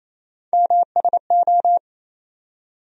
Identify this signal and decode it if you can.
Morse code, heard as a keyed tone, about 28 wpm, 700 Hz. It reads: MSO
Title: USA Call Sign Postfixes - 3 Letters 28wpm